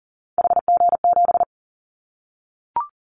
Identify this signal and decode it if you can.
Morse code, heard as a keyed tone, about 40 wpm, 700 Hz. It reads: HG7
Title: Callsign Prefixes for Top 25 DXCC - Repeated 3x - 40wpm